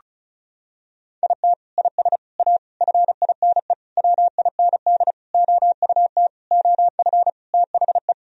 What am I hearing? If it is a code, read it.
Morse code, heard as a keyed tone, about 36 wpm, 700 Hz. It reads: IT IS A FINE WIND OUT OF THE
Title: Sentences from Top 1000 Words - Review Part 2 - 35wpm